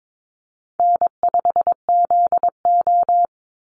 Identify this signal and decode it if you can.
Morse code, heard as a keyed tone, about 22 wpm, 700 Hz. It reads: N5ZO
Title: Easy Call Signs Repeated 2x - 22wpm